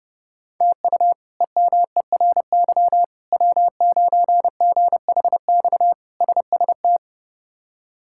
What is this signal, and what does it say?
Morse code, heard as a keyed tone, about 32 wpm, 700 Hz. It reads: TU EMERY W9GHX SST
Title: K1USN SST QSO Practice - 30wpm